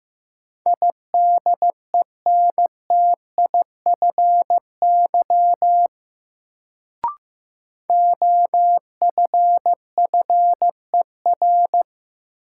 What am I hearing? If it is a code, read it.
Morse code, heard as a keyed tone, about 15 wpm, 700 Hz. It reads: IDENTIFY OFFER
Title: Sentences from Top 900 Words - New Word Drill - 15wpm